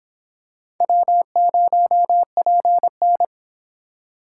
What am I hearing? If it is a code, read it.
Morse code, heard as a keyed tone, about 26 wpm, 700 Hz. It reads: W0PN